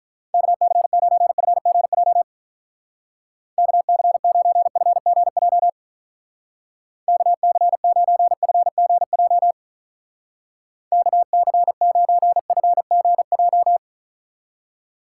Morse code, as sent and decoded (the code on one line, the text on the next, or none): -.- -.-. ----. ..-. --. .--- / -.- -.-. ----. ..-. --. .--- / -.- -.-. ----. ..-. --. .--- / -.- -.-. ----. ..-. --. .---
KC9FGJ KC9FGJ KC9FGJ KC9FGJ